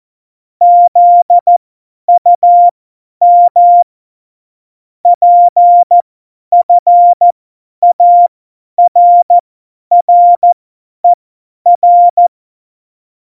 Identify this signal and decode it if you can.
Morse code, heard as a keyed tone, about 14 wpm, spaced as for 10 wpm, 700 Hz. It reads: ZUM PFARRER
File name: Morsekurs / Witze / Witz 08. Dezember 2020, 50 F70